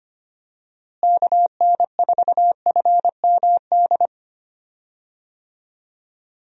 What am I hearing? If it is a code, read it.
Morse code, heard as a keyed tone, about 25 wpm, 700 Hz. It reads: KN4FMD
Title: USA Call Signs - 6 Characters - Repeated 3x - 25wpm